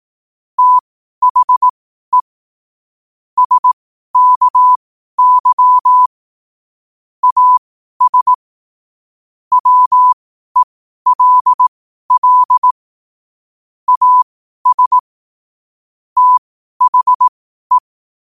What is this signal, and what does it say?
Morse code, heard as a keyed tone, about 18 wpm, spaced as for 12 wpm, 1000 Hz. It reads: THE SKY AS WELL AS THE
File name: qotdfiles / 12wpm